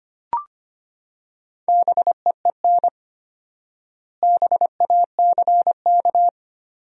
Morse code, heard as a keyed tone, about 25 wpm, 700 Hz. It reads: BEEN BACK
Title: 4x Word-Spacing - Sentences from Top 200 Words - 25wpm